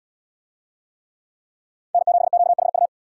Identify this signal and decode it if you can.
Morse code, heard as a keyed tone, about 73 wpm, 700 Hz. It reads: N7QHU